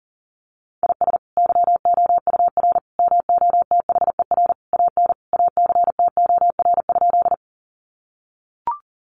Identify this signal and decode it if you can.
Morse code, heard as a keyed tone, about 40 wpm, 700 Hz. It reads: IS YOUR MOTHER AN ACTOR?
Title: Sentences from Top 1500 Words 40wpm